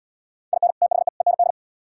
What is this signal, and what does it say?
Morse code, heard as a keyed tone, about 76 wpm, 700 Hz. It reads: IN THE END